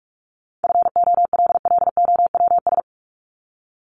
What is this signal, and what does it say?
Morse code, heard as a keyed tone, about 44 wpm, 700 Hz. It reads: FOLLOWS
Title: ES:S Suffix - Top 100 Words with Base Words 45wpm